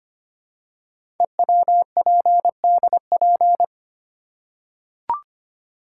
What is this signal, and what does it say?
Morse code, heard as a keyed tone, about 25 wpm, 700 Hz. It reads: EWPDP
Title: Code Groups - 5 Letters - 25wpm